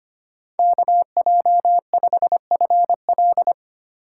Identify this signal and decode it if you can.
Morse code, heard as a keyed tone, about 32 wpm, 700 Hz. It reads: KJ5FL